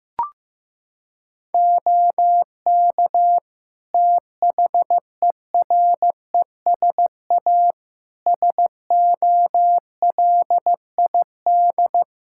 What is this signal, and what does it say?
Morse code, heard as a keyed tone, about 15 wpm, 700 Hz. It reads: OK THERESA SOLID
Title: Practice QSO - Exchange 3 - More Info - 15wpm